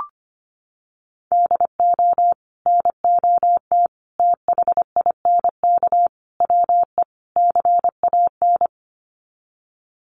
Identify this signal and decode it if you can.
Morse code, heard as a keyed tone, about 25 wpm, 700 Hz. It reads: DO NOT THINK WE CAN
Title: Sets of 5 Words Encore2 25wpm